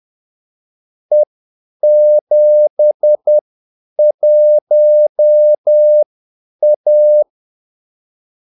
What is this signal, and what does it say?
Morse code, heard as a keyed tone, about 10 wpm, spaced as for 8 wpm, 600 Hz. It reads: E71A
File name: Callsigns-10-8wpm